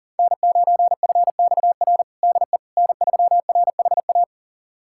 Morse code, heard as a keyed tone, about 40 wpm, 690 Hz. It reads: N9FXR DE N3RHA
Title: USA Callsign DE Callsign - 5 Characters - 40wpm